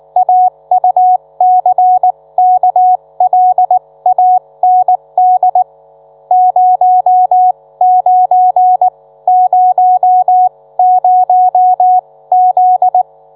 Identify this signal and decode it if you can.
Morse code, heard as a keyed tone, about 19 wpm, spaced as for 16 wpm, 730 Hz. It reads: AUCKLAND 0900Z